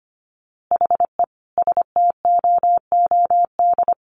Morse code, heard as a keyed tone, about 25 wpm, 700 Hz. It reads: HE STOOD
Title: A Sound Of Thunder 25wpm